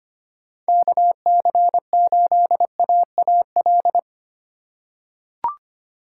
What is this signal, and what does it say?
Morse code, heard as a keyed tone, about 25 wpm, 700 Hz. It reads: KC8AAL
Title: USA Call Signs - 6 Characters - 3x Character Spacing - 25wpm